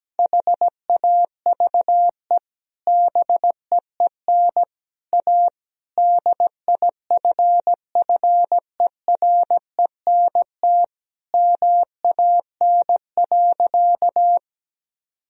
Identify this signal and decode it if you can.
Morse code, heard as a keyed tone, about 17 wpm, 700 Hz. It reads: HAVE BEEN A DIFFERENT MAN.
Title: Sentences from Top 1000 Words - Review Part 3 - 17wpm